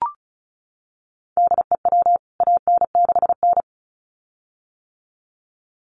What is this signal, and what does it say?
Morse code, heard as a keyed tone, about 34 wpm, 700 Hz. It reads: DEW AN6N
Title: CWT Exchange 35wpm